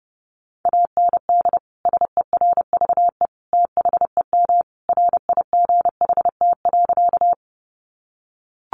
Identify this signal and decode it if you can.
Morse code, heard as a keyed tone, about 30 wpm, 700 Hz. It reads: AND SERVE THEM RIGHT.
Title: Sentences from Top 400 Words 30wpm